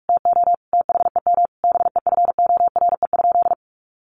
Morse code, heard as a keyed tone, about 44 wpm, 700 Hz. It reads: TO THEM BEFORE?
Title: Sentences from Top 900 Words 45wpm